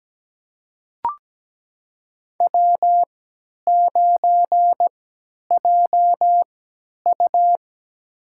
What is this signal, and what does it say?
Morse code, heard as a keyed tone, about 17 wpm, spaced as for 10 wpm, 700 Hz. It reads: W9JU
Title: USA Call Signs - 4 Characters - 3x Character Spacing - 17wpm